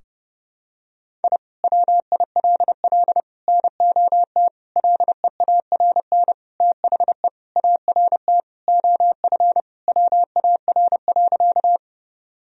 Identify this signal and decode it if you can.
Morse code, heard as a keyed tone, about 30 wpm, 700 Hz. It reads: I WILL NOT LEARN THE ART OF WAR.
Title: Sentences from Top 1000 Words - Review Part 3 - 30wpm